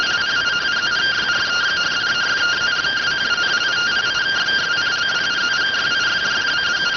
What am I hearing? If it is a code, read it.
Two-tone teleprinter signal, about 170 Hz shift.